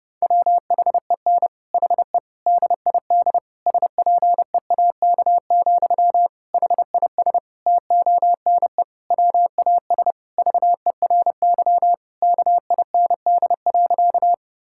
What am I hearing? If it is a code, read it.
Morse code, heard as a keyed tone, about 30 wpm, 700 Hz. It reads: WHEN HE DID SPEAK, HIS TONE WAS VERY KIND.
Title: Sentences from Top 1600 Words 30wpm